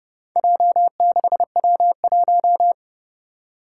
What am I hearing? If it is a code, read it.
Morse code, heard as a keyed tone, about 32 wpm, 700 Hz. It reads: J6W1
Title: Four Challenging Letter-Number Combinations 30wpm